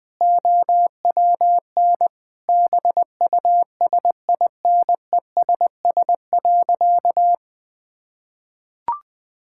Morse code, heard as a keyed tone, about 20 wpm, 700 Hz. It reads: OWN BUSINESS.
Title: Sentences from Top 1000 Words - Review Part 5 - 20wpm